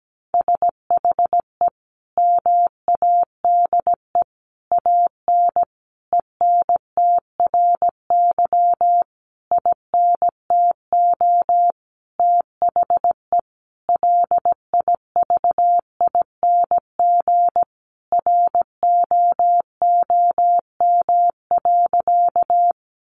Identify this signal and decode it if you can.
Morse code, heard as a keyed tone, about 17 wpm, 700 Hz. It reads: SHE MADE AN ENTRY INTO THE LIVING ROOM.
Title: Sentences from Top 1500 Words 17wpm